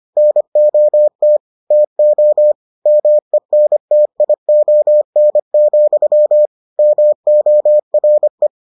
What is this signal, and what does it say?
Morse code, heard as a keyed tone, about 25 wpm, 600 Hz. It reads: NOT TO MENTION, MORE